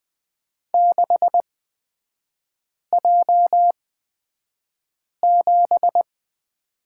Morse code, heard as a keyed tone, about 20 wpm, spaced as for 5 wpm, 700 Hz. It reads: 6J7
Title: 255 Lesson 41e - Period - Three Characters 20wpm